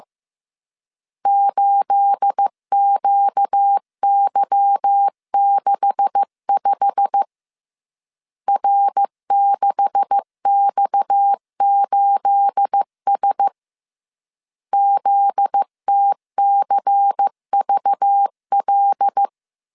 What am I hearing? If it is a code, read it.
Morse code, heard as a keyed tone, about 15 wpm, 800 Hz. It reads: 8QY65 R6X8S ZTCVL
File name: CW / DET / 1 9 1500C S A 14wpm 3222